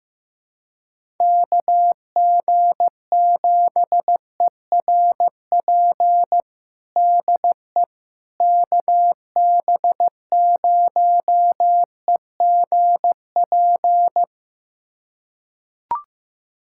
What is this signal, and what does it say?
Morse code, heard as a keyed tone, about 15 wpm, 700 Hz. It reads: KG7ERP DE KB0EGP
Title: USA Callsign DE Callsign 15wpm